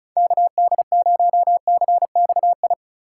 Morse code, heard as a keyed tone, about 36 wpm, 700 Hz. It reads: KD0CXI
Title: USA Call Signs - 4x Character Spacing - 35wpm